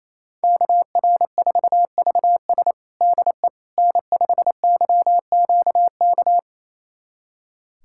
Morse code, heard as a keyed tone, about 28 wpm, 700 Hz. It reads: KR4VS DE N5YQK
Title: USA Callsign DE Callsign - 5 Characters - 28wpm